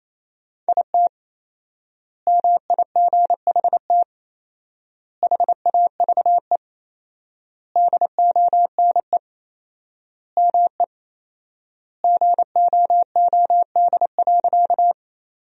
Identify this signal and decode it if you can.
Morse code, heard as a keyed tone, about 28 wpm, 700 Hz. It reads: IT MIGHT HAVE DONE ME GOOD.
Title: 4x Word-Spacing - Sentences from Top 500 Words - 28wpm